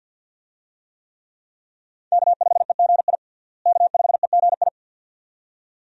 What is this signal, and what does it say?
Morse code, heard as a keyed tone, about 51 wpm, 700 Hz. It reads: K5EGI K5EGI
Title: USA Call Signs - 5 Characters - Repeated 2x - 50wpm